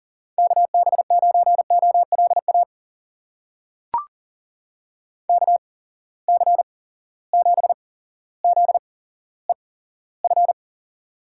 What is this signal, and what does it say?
Morse code, heard as a keyed tone, about 39 wpm, spaced as for 17 wpm, 700 Hz. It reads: KB9OLA KC7ZEF
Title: USA Call Signs - 6 Characters - 8x Character Spacing - 40wpm